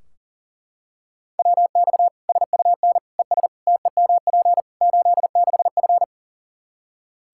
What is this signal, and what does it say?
Morse code, heard as a keyed tone, about 37 wpm, 700 Hz. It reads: WX SUN ES TEMP 86F
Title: K1USN SST QSO Practice 40wpm